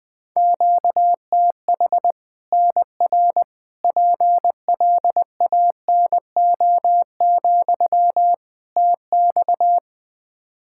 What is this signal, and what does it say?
Morse code, heard as a keyed tone, about 20 wpm, 700 Hz. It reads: QTH NR PLANO, TX
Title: Practice QSO - Exchange 1 - Beginning - 20wpm